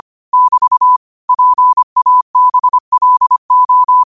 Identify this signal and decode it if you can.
Morse code, heard as a keyed tone, about 25 wpm, 1000 Hz. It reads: <BT> PABLO
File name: qotdfiles / 25wpm